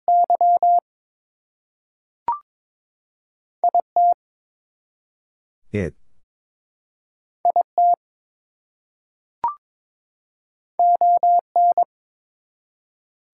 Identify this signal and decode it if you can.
Morse code, heard as a keyed tone, about 22 wpm, 700 Hz. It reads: Y IT IT ON
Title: Two Letter Words - 22wpm